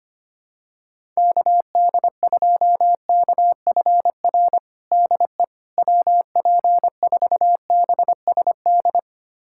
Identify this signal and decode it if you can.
Morse code, heard as a keyed tone, about 25 wpm, 700 Hz. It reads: KD2KFR DE WP4BSD
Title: USA Callsign DE Callsign 25wpm